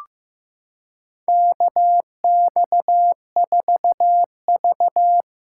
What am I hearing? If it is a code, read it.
Morse code, heard as a keyed tone, about 15 wpm, 700 Hz. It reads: KX4V
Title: USA Callsign DE Callsign 15wpm